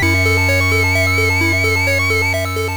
There is no code.